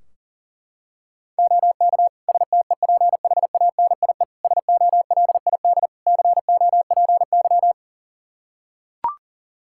Morse code, heard as a keyed tone, about 40 wpm, 700 Hz. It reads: OK STEPHANIE SOLID COPY
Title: Practice QSO - Exchange 3 - More Info - 40wpm